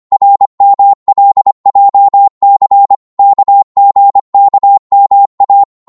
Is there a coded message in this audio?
Yes: Morse code from 0.1 to 5.6 s, heard as a keyed tone, about 25 wpm, 800 Hz. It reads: RMLJC KGKMA